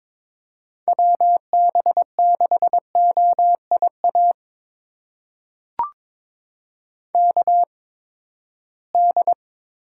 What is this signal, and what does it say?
Morse code, heard as a keyed tone, about 22 wpm, spaced as for 15 wpm, 700 Hz. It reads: WB6OIA KD
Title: USA Call Signs - 8x Character Spacing - 22wpm